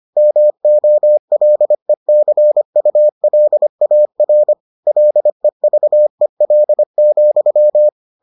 Morse code, heard as a keyed tone, about 25 wpm, 600 Hz. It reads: MOLECULAR LEVEL,